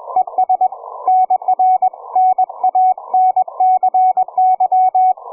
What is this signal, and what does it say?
Morse code, heard as a keyed tone, about 21 wpm, 740 Hz, with radio noise. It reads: ES NR NANCY
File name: cw forum / ve1zj 1830 160m 04h00